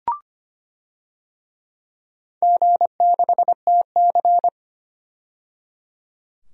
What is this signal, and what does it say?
Morse code, heard as a keyed tone, about 25 wpm, 700 Hz. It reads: G6TC
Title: DXCC-100 Call Signs - 4 Characters - 25wpm